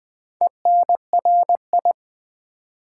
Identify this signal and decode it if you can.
Morse code, heard as a keyed tone, about 20 wpm, 700 Hz. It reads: ENRI
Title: Build A Word - 2 Letters At A Time - 6-Letter Words - 20wpm